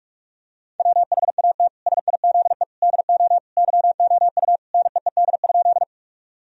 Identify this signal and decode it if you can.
Morse code, heard as a keyed tone, about 46 wpm, 700 Hz. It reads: WHAT SIZE DO YOU NEED?
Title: Sentences from Top 1000 Words - Review Part 2 - 45wpm